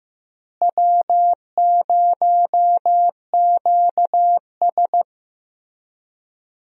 Morse code, heard as a keyed tone, about 15 wpm, 700 Hz. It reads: W0QS
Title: USA Call Signs - 4 Characters - Repeated 3x - 15wpm